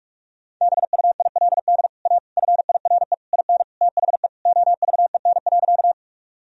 Morse code, heard as a keyed tone, about 41 wpm, 700 Hz. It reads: BUILD A FIRE IN THE OVEN.
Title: Sentences from Top 1000 Words - Review Part 4 - 45wpm